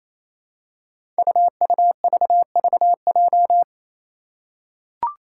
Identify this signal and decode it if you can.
Morse code, heard as a keyed tone, about 28 wpm, 700 Hz. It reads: UUVVJ
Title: Five Challenging Letter-Number Combinations 28wpm